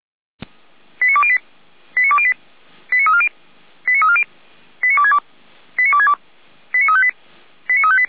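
Five-tone selective call (ZVEI), heard as quick runs of five tones, tones of about 70 ms. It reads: 89208 89208 89340 89340 89262 89262 89358 89358
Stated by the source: Five-tone ZVEI selective call received on SDR.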